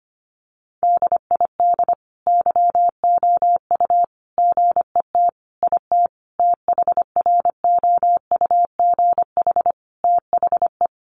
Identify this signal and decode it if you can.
Morse code, heard as a keyed tone, about 25 wpm, 700 Hz. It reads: DID YOU GET IT THROUGH THE
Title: Sentences from Top 1300 Words 25wpm